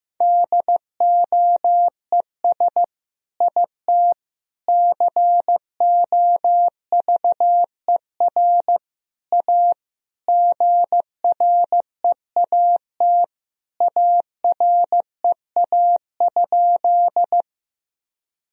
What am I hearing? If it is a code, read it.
Morse code, heard as a keyed tone, about 15 wpm, 700 Hz. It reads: DOES IT COVER A GREAT AREA?
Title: Sentences from Top 1000 Words - Review Part 2 - 15wpm